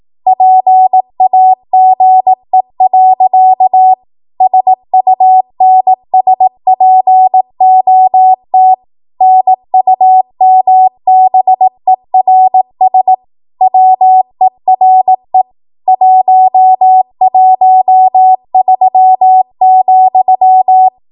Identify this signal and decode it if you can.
Morse code, heard as a keyed tone, about 18 wpm, 750 Hz. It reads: PAGE. SUNSPOT NUMBERS WERE 113,
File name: Archive / 18 WPM / 18WPM